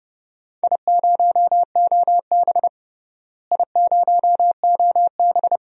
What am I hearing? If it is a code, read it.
Morse code, heard as a keyed tone, about 30 wpm, 700 Hz. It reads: I0OB I0OB
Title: DXCC-100 Call Signs - 4 Characters - Repeated 3x - 30wpm